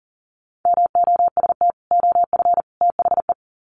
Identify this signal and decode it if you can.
Morse code, heard as a keyed tone, about 40 wpm, 700 Hz. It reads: MOST OF THE
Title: Sets of 3 Words 40wpm